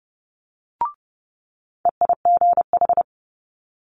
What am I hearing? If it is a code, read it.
Morse code, heard as a keyed tone, about 30 wpm, 700 Hz. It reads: EIGH
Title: Multi-letter Phonemes Quick Fire 30wpm